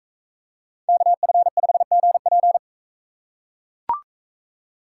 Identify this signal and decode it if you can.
Morse code, heard as a keyed tone, about 41 wpm, 700 Hz. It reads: KF5GP